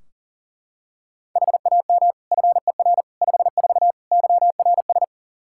Morse code, heard as a keyed tone, about 40 wpm, 700 Hz. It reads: HAM FER 54 YRS
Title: Practice QSO - Exchange 3 - More Info - 40wpm